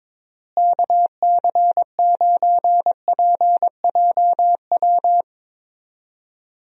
Morse code, heard as a keyed tone, about 22 wpm, 700 Hz. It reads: KC9PJW
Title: USA Call Signs - 6 Characters - Repeated 3x - 22wpm